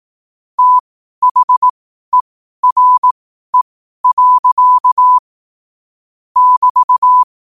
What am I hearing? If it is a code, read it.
Morse code, heard as a keyed tone, about 18 wpm, spaced as for 12 wpm, 1000 Hz. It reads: THERE. <BT>
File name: qotdfiles / 12wpm